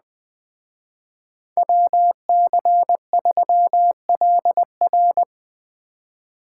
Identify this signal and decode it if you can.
Morse code, heard as a keyed tone, about 20 wpm, 700 Hz. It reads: WC3LR
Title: Speed Racing - USA 2x2 Call Signs 17wpm